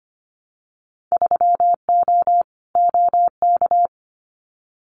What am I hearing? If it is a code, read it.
Morse code, heard as a keyed tone, about 25 wpm, 700 Hz. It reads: 3O OK
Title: Winter Field Day Exchanges Encore 25wpm